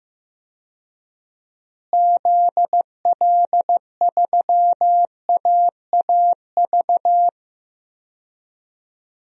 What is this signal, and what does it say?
Morse code, heard as a keyed tone, about 16 wpm, 700 Hz. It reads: ZL3AAV
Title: DXCC-100 Call Signs - 6 Characters - Repeated 2x - 15wpm